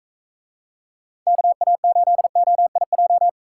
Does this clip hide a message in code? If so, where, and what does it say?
Yes: Morse code from 1.3 to 3.3 s, heard as a keyed tone, about 44 wpm, 700 Hz. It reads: KA8OIJ